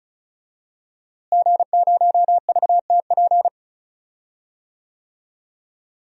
Morse code, heard as a keyed tone, about 36 wpm, 700 Hz. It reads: G0VTP
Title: DXCC-100 Call Signs - 5 Characters - Repeated 3x - 35wpm